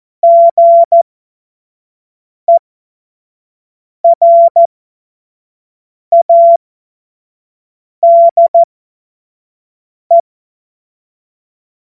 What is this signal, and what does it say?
Morse code, heard as a keyed tone, about 14 wpm, spaced as for 5 wpm, 680 Hz. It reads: GERADE
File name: Morsekurs / Witze / Witz 08. Februar 2022 25 F70